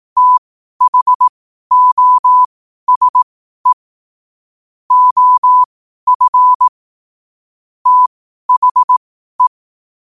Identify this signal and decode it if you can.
Morse code, heard as a keyed tone, about 18 wpm, spaced as for 12 wpm, 1000 Hz. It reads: THOSE OF THE
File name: qotdfiles / 12wpm